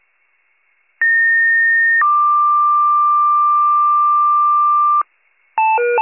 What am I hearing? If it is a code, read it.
Two-tone page, heard as a one-second tone, then a three-second tone, about 1850 then 1180 Hz. It is heard from 1.0 to 5.0 s.